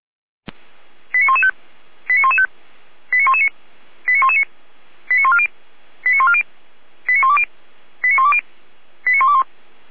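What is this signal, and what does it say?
Five-tone selective call (ZVEI), heard as quick runs of five tones, tones of about 70 ms. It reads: 89115 89115 89119 89119 89140 89140 89120 89120 89121
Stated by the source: ZVEI five-tone selective call received with an SDR.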